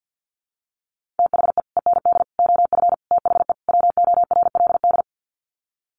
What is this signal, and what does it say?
Morse code, heard as a keyed tone, about 50 wpm, 700 Hz. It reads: THE END OF THE WORLD
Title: Sets of 5 Words 50wpm